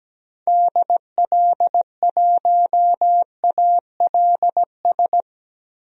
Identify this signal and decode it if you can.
Morse code, heard as a keyed tone, about 17 wpm, 700 Hz. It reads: DL1ALS
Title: DXCC-100 Call Signs - 6 Characters - Repeated 3x - 17wpm